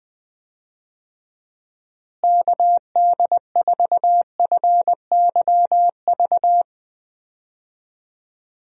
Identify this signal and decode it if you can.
Morse code, heard as a keyed tone, about 21 wpm, 700 Hz. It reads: KD4FYV
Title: USA Call Signs - 6 Characters - Repeated 3x - 20wpm